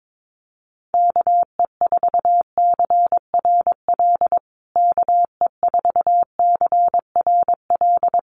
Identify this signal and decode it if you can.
Morse code, heard as a keyed tone, about 22 wpm, 700 Hz. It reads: KE4CRL KE4CRL
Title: USA Call Signs Repeated 3x 22wpm